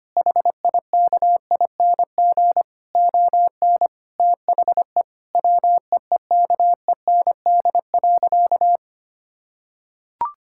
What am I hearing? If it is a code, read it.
Morse code, heard as a keyed tone, about 25 wpm, 700 Hz. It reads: HIKING ON THE WEEKEND.
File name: practice / 25wpm / Difficult Sentences with Hints - 25wpm